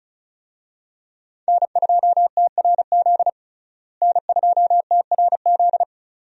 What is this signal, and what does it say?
Morse code, heard as a keyed tone, about 34 wpm, 700 Hz. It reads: N2TRZ N2TRZ
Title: USA Call Signs - 5 Characters - Repeated 2x - 35wpm